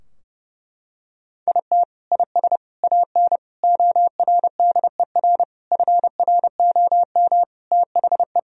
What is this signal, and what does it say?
Morse code, heard as a keyed tone, about 30 wpm, 700 Hz. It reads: IT IS AN ORDER FROM THE
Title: Sentences from Top 1000 Words - Review Part 2 - 30wpm